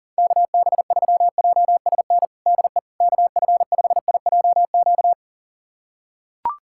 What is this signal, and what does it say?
Morse code, heard as a keyed tone, about 40 wpm, 700 Hz. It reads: KB3JSN DE KF5IJQ
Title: USA Callsign DE Callsign - 6 Characters - 40wpm